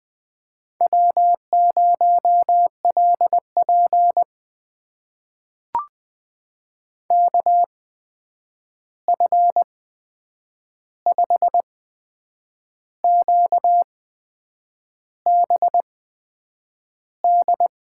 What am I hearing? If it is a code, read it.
Morse code, heard as a keyed tone, about 20 wpm, spaced as for 8 wpm, 700 Hz. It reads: W0LP KF5QBD
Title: USA Call Signs - 8x Character Spacing - 20wpm